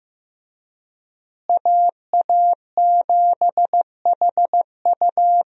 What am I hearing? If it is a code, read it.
Morse code, heard as a keyed tone, about 15 wpm, 700 Hz. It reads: AA7HU
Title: USA Call Signs - 5 Characters - Repeated 2x - 15wpm